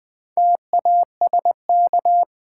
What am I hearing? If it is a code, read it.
Morse code, heard as a keyed tone, about 20 wpm, 700 Hz. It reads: TASK
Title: Sentences from Top 1000 Words - Review Part 7 - New Word Drill - 20wpm